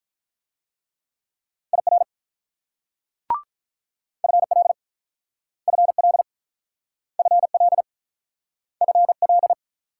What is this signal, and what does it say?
Morse code, heard as a keyed tone, about 43 wpm, 700 Hz. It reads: IR FL FL FL FL
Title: Speed-Racing Multi-letter Phonemes - 35wpm